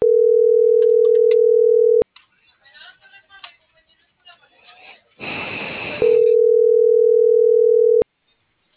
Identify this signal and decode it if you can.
Ringback tone (North America), 440 + 480 Hz.
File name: CENTRAL MABE / out miriam.soledad - .63230